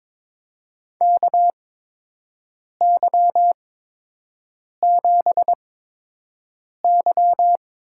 Morse code, heard as a keyed tone, about 22 wpm, spaced as for 6 wpm, 700 Hz. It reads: KY7Y
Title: USA Call Signs - 8x Character Spacing - 22wpm